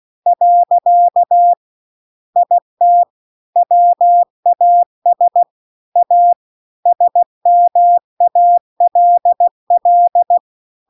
Morse code, heard as a keyed tone, about 16 wpm, 700 Hz. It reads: . IT WAS A SMALL
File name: CW-The-Big-Twoheaded-River-2-16WPM